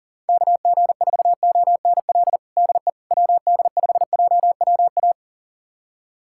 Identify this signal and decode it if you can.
Morse code, heard as a keyed tone, about 40 wpm, 700 Hz. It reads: KG4ONL DE WD5JWA
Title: USA Callsign DE Callsign - 6 Characters - 40wpm